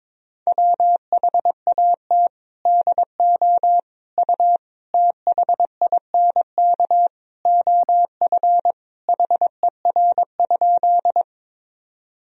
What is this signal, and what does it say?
Morse code, heard as a keyed tone, about 22 wpm, 700 Hz. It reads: WHAT DO U THINK OF HER?
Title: Sentences from Top 200 Words and Abbreviations 22wpm